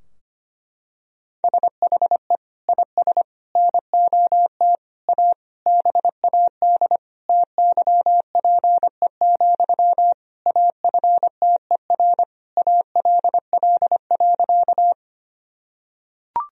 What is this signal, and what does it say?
Morse code, heard as a keyed tone, about 25 wpm, 700 Hz. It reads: SHE IS NOT A BAD TYPE, AFTER ALL.
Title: Sentences from Top 1000 Words - Review Part 3 - 25wpm